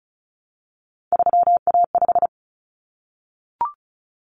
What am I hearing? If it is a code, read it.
Morse code, heard as a keyed tone, about 34 wpm, 700 Hz. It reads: 3A5
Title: Common Cut Numbers - 3-Digit 35wpm